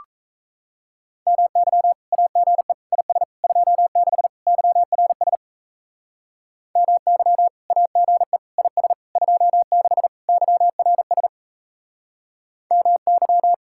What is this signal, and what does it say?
Morse code, heard as a keyed tone, about 34 wpm, 700 Hz. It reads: MY AGE IS 26 YRS MY AGE IS 26 YRS MY
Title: Speed-Racing Common QSO Phrases - 28wpm